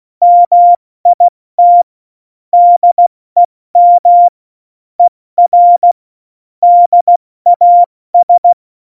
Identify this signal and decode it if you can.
Morse code, heard as a keyed tone, about 16 wpm, 700 Hz. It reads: MIT DEM ER DAS
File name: Morsekurs / Witze / Witz 26 Mai, 2020 70 F80